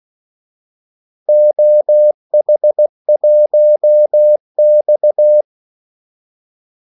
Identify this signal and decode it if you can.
Morse code, heard as a keyed tone, about 16 wpm, 600 Hz. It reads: OH1X
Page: Kaldesignaler 16 wpm